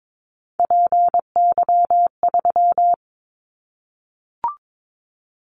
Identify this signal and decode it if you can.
Morse code, heard as a keyed tone, about 22 wpm, 700 Hz. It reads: PY3
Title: Callsign Prefixes for Top 100 DXCC - 22wpm